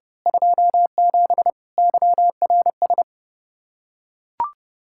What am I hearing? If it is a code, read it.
Morse code, heard as a keyed tone, about 29 wpm, 700 Hz. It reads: 27 YRS
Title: Practice QSO - Exchange 3 - More Info - 30wpm